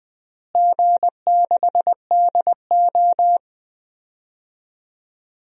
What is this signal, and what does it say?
Morse code, heard as a keyed tone, about 20 wpm, 700 Hz. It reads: G6DO
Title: DXCC-100 Call Signs - 4 Characters - Repeated 2x - 20wpm